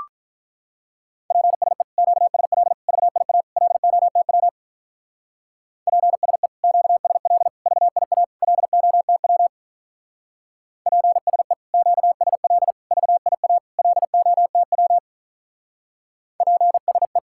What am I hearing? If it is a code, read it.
Morse code, heard as a keyed tone, about 44 wpm, 700 Hz. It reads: PSE QSL VIA LOTW PSE QSL VIA LOTW PSE QSL VIA LOTW PSE